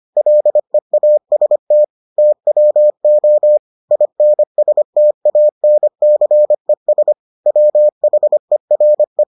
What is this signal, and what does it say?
Morse code, heard as a keyed tone, about 25 wpm, 600 Hz. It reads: LEAST TWO INSTANCES WHERE